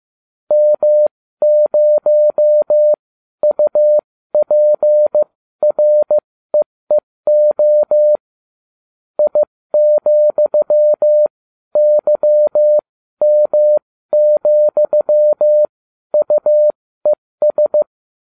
Morse code, heard as a keyed tone, about 15 wpm, spaced as for 12 wpm, 600 Hz. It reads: M0UPREEO I,YM,UES